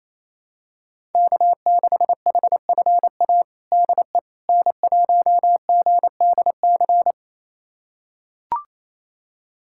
Morse code, heard as a keyed tone, about 28 wpm, 700 Hz. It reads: K6HFA DE N1GDC
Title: USA Callsign DE Callsign - 5 Characters - 28wpm